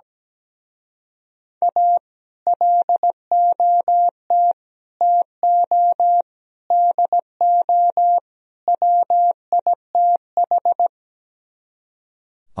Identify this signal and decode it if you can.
Morse code, heard as a keyed tone, about 17 wpm, 700 Hz. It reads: A LOT TO DO WITH
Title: Speed-Racing - Sets of 5 Words 17wpm